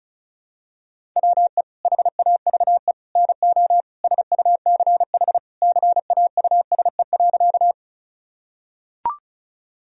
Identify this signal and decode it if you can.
Morse code, heard as a keyed tone, about 36 wpm, 700 Hz. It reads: WE HAVE NO SUCH CAUSE.
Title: Sentences from Top 1000 Words - Review Part 1 - 35wpm